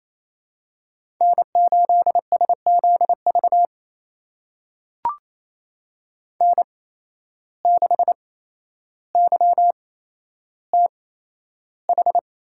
Morse code, heard as a keyed tone, about 28 wpm, spaced as for 12 wpm, 700 Hz. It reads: N8SZV N6YTH
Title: USA Call Signs - 5 Characters - 8x Character Spacing - 28wpm